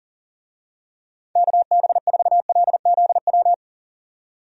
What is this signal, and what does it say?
Morse code, heard as a keyed tone, about 40 wpm, 700 Hz. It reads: KB4LZW
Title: USA Call Signs - 6 Characters - 40wpm